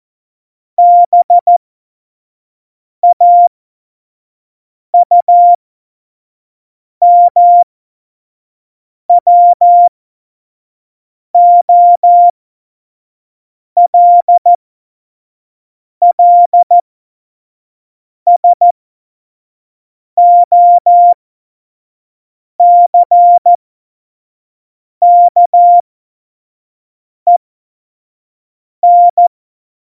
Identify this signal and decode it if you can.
Morse code, heard as a keyed tone, about 14 wpm, spaced as for 5 wpm, 700 Hz. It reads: BAUMWOLLSOCKEN
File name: Morsekurs / Witze / Witz 06 March 2018, 25 F70